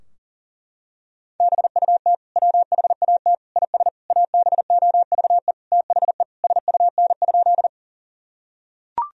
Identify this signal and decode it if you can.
Morse code, heard as a keyed tone, about 37 wpm, 700 Hz. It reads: BUT WHAT IS ABOVE THE SUN?
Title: Sentences from Top 1000 Words - Review Part 2 - 40wpm